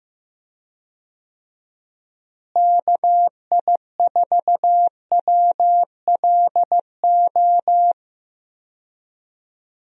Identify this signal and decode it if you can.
Morse code, heard as a keyed tone, about 15 wpm, 700 Hz. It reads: KI4WLO
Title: USA Call Signs - 6 Characters - Repeated 2x - 15wpm